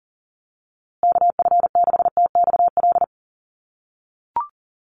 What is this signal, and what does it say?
Morse code, heard as a keyed tone, about 40 wpm, 700 Hz. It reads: KF6TXL
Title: USA Call Signs Repeated 3x 40wpm